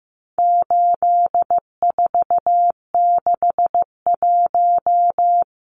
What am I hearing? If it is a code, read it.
Morse code, heard as a keyed tone, about 15 wpm, 700 Hz. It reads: 8461
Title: Numbers - 4-Digit 15wpm